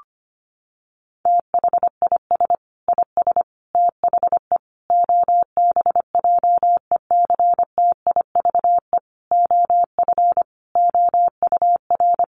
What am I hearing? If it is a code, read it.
Morse code, heard as a keyed tone, about 25 wpm, 700 Hz. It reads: THIS IS THE OBJECTIVE OF OUR
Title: Sentences from Top 1200 Words 25wpm